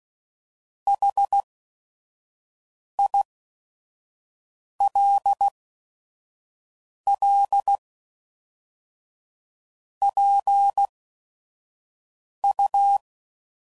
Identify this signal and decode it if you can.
Morse code, heard as a keyed tone, about 16 wpm, spaced as for 5 wpm, 800 Hz. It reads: HILL PU
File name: CW Code Groups @5WPM / QSOs / kc7haa